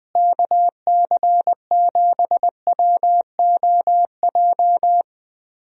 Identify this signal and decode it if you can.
Morse code, heard as a keyed tone, about 20 wpm, 700 Hz. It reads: KC7WOJ
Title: USA Call Signs - 6 Characters - Repeated 3x - 20wpm